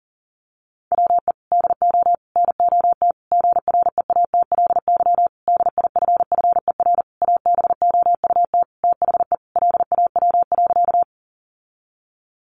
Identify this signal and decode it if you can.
Morse code, heard as a keyed tone, about 40 wpm, 700 Hz. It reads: WE DO NOT GREATLY DIFFER ABOUT THE LAW.
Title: Sentences from Top 1400 Words 40wpm